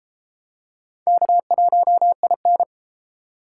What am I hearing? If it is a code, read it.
Morse code, heard as a keyed tone, about 30 wpm, 700 Hz. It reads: K1IN